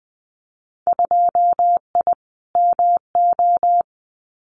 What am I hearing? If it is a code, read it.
Morse code, heard as a keyed tone, about 20 wpm, 700 Hz. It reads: 2I MO
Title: Winter Field Day Exchanges 20wpm